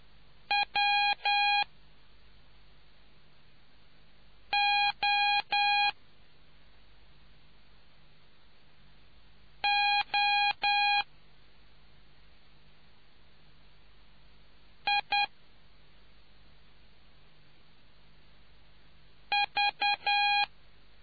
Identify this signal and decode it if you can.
Morse code, heard as a keyed tone, about 10 wpm, spaced as for 2 wpm, 820 Hz. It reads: WOOIV